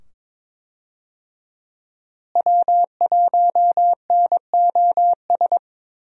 Morse code, heard as a keyed tone, about 22 wpm, 700 Hz. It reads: W1NOS
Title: USA Call Signs - 5 Characters - Repeated 2x - 22wpm